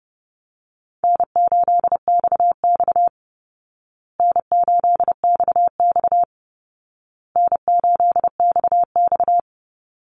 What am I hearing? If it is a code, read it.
Morse code, heard as a keyed tone, about 30 wpm, 700 Hz. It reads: N8XX N8XX N8XX
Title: Call Signs Repeated 2x - 30wpm